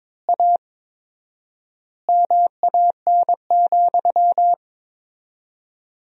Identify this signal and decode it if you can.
Morse code, heard as a keyed tone, about 23 wpm, 700 Hz. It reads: A MAN,
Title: 4x Word-Spacing - Sentences from Top 200 Words - 22wpm